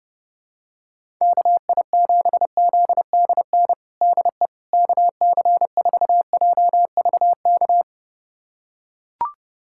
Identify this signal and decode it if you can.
Morse code, heard as a keyed tone, about 32 wpm, 700 Hz. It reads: KI7ZDN DE KC4JVK
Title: USA Callsign DE Callsign - 6 Characters - 30wpm